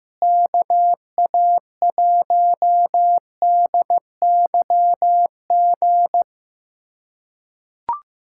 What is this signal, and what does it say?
Morse code, heard as a keyed tone, about 16 wpm, 700 Hz. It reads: KA1DYG
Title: USA Callsign DE Callsign - 6 Characters - 15wpm